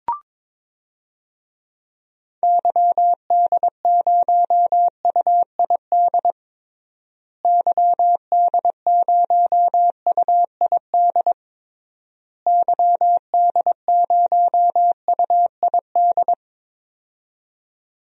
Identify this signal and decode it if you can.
Morse code, heard as a keyed tone, about 22 wpm, 700 Hz. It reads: YD0UID YD0UID YD0UID
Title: DXCC-100 Call Signs - 6 Characters - Repeated 3x - 22wpm